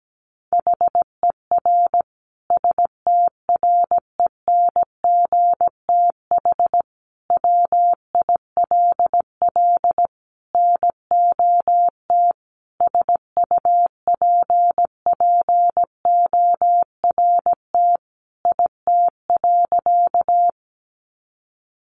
Morse code, heard as a keyed tone, about 17 wpm, 700 Hz. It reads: HER STRENGTH WILL NOT SUPPORT IT.
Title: Sentences from Top 1300 Words 17wpm